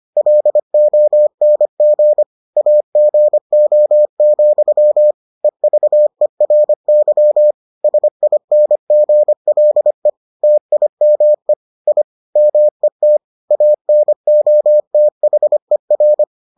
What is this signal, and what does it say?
Morse code, heard as a keyed tone, about 25 wpm, 600 Hz. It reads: LONG AGO, EVERY SINGLE TIME I MET ANOTHER